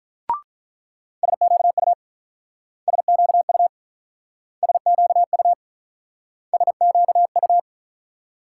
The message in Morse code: SQU SQU SQU SQU

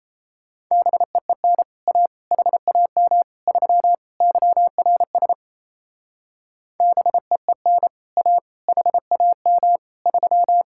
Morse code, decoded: BEEN A HAM 3 YRS BEEN A HAM 3

